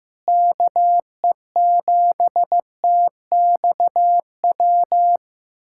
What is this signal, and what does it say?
Morse code, heard as a keyed tone, about 16 wpm, 700 Hz. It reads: KE7TXW
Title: USA Callsign DE Callsign 15wpm